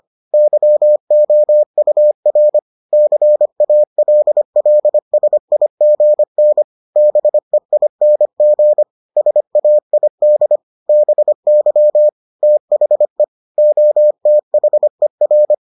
Morse code, heard as a keyed tone, about 25 wpm, 600 Hz. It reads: YOUR CALLSIGN BEING SAID BY THE OTHER